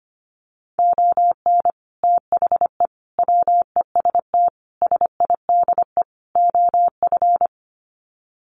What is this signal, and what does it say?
Morse code, heard as a keyed tone, about 25 wpm, 700 Hz. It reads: ON THE WEST SIDE OF
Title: Sets of 5 Words Encore2 25wpm